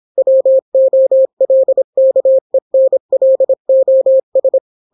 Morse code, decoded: WOLKENLOS